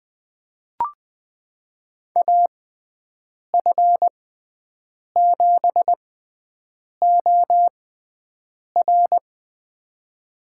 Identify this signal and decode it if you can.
Morse code, heard as a keyed tone, about 21 wpm, spaced as for 7 wpm, 700 Hz. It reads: AF7OR
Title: USA Call Signs - 5 Characters - 6x Character Spacing - 20wpm